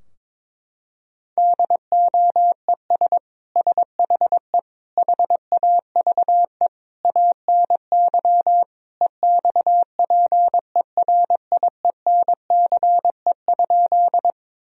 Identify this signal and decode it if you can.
Morse code, heard as a keyed tone, about 22 wpm, 700 Hz. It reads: DOES SHE HAVE ANY EXPERIENCE?
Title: Sentences from Top 1000 Words - Review Part 3 - 22wpm